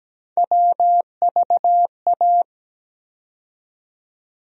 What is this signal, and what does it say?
Morse code, heard as a keyed tone, about 17 wpm, 700 Hz. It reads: WVA
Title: USA Call Sign Postfixes - 3 Letters 17wpm